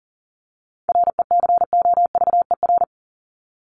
Morse code, heard as a keyed tone, about 40 wpm, 700 Hz. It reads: RECOVER
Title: Top 1751-2000 Words 40wpm